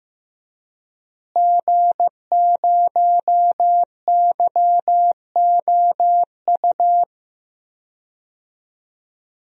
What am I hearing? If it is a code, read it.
Morse code, heard as a keyed tone, about 15 wpm, 700 Hz. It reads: G0YOU
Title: DXCC-100 Call Signs - 5 Characters - Repeated 2x - 15wpm